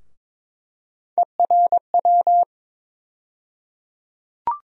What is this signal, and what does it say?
Morse code, heard as a keyed tone, about 22 wpm, 700 Hz. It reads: ERW
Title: USA Call Sign Postfixes - 3 Letters 22wpm